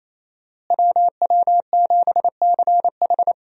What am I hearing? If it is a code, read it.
Morse code, heard as a keyed tone, about 28 wpm, 700 Hz. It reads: WW7CH
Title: USA Call Signs - 5 Characters - Repeated 2x - 28wpm